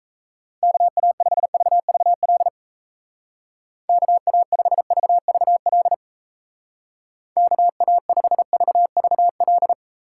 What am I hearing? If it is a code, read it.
Morse code, heard as a keyed tone, about 34 wpm, 700 Hz. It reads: KA5VVL KA5VVL KA5VVL
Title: Speed Racing - USA Call Signs 28wpm